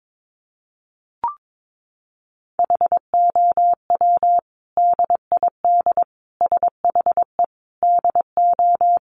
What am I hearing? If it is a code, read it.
Morse code, heard as a keyed tone, about 22 wpm, 700 Hz. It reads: HOW DID SHE DO
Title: Sentences from Top 100 Words 22wpm